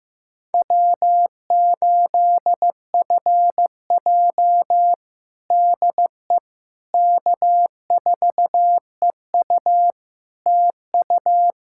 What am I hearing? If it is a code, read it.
Morse code, heard as a keyed tone, about 15 wpm, 700 Hz. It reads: W8FJ DE K4EU TU
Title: Practice QSO - Exchange 4 - Ending - 15wpm